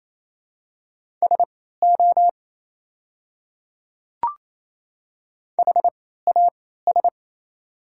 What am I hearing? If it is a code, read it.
Morse code, heard as a keyed tone, about 28 wpm, spaced as for 16 wpm, 700 Hz. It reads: SO HAS
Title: 3x Character Spacing - Top 100 Words - 28wpm